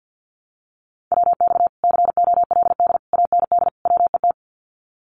Morse code, heard as a keyed tone, about 56 wpm, 700 Hz. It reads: WX COLD AND WET
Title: Common QSO Phrases Encore 50wpm